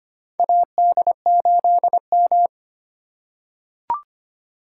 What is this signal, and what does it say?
Morse code, heard as a keyed tone, about 25 wpm, 700 Hz. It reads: AD8M
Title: USA Call Signs - 4 Characters - 3x Character Spacing - 25wpm